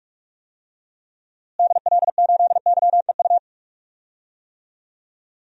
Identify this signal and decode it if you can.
Morse code, heard as a keyed tone, about 46 wpm, 700 Hz. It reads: DL8YEU
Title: DXCC-100 Call Signs - 6 Characters - Repeated 2x - 45wpm